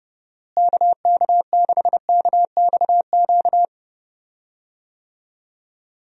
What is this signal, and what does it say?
Morse code, heard as a keyed tone, about 28 wpm, 700 Hz. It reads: KK6KXQ
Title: USA Call Signs - 6 Characters - Repeated 3x - 30wpm